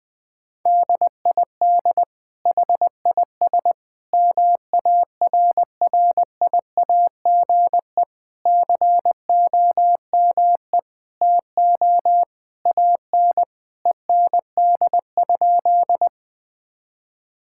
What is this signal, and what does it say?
Morse code, heard as a keyed tone, about 20 wpm, 700 Hz. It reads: DID HIS MARRIAGE COME TO AN END?
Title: Sentences from Top 1000 Words - Review Part 5 - 20wpm